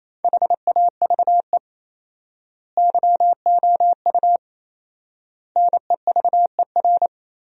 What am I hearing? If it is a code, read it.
Morse code, heard as a keyed tone, about 28 wpm, 700 Hz. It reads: HAVE YOU NEVER
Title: 4x Word-Spacing - Sentences from Top 300 Words - 28wpm